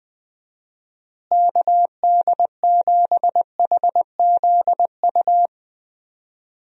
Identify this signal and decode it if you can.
Morse code, heard as a keyed tone, about 20 wpm, 700 Hz. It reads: KD7HZU
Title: USA Call Signs - 6 Characters - Repeated 3x - 20wpm